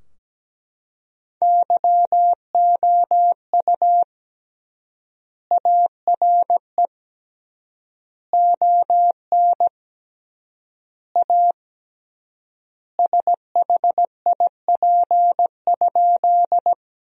Morse code, heard as a keyed tone, about 17 wpm, 700 Hz. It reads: YOU ARE ON A SHIP?
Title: 3x Word-Spacing - Sentences from Top 400 Words - 17wpm